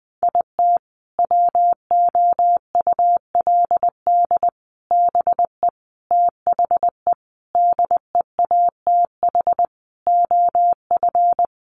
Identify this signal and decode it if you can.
Morse code, heard as a keyed tone, about 20 wpm, 700 Hz. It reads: IT WOULD BE THE DEATH OF
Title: Sentences from Top 1400 Words 20wpm